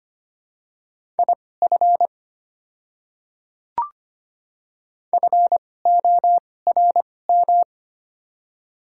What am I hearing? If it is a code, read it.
Morse code, heard as a keyed tone, about 25 wpm, spaced as for 18 wpm, 700 Hz. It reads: IF FORM
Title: 2x Character Spacing - Top 200 Words - 25wpm